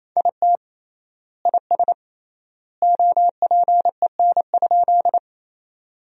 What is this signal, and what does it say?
Morse code, heard as a keyed tone, about 28 wpm, 700 Hz. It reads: IT IS OPEN?
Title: 3x Word-Spacing - Sentences from Top 300 Words - 28wpm